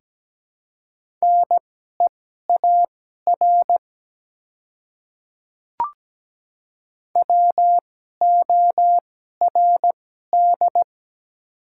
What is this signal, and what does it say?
Morse code, heard as a keyed tone, about 17 wpm, spaced as for 12 wpm, 700 Hz. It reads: NEAR WORD
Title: 2x Character Spacing - Top 200 Words - 17wpm